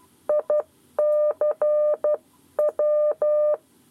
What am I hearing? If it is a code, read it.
Morse code, heard as a keyed tone, about 12 wpm, spaced as for 10 wpm, 600 Hz. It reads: ICW
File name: 7 Lezione cw 1